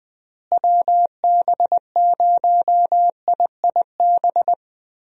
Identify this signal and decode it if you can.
Morse code, heard as a keyed tone, about 20 wpm, 700 Hz. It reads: WB0IIB
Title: USA Call Signs - 6 Characters - Repeated 3x - 20wpm